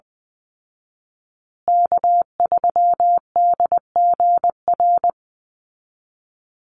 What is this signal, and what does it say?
Morse code, heard as a keyed tone, about 20 wpm, 700 Hz. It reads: K3DGR
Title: Call Signs Repeated 2x - 20wpm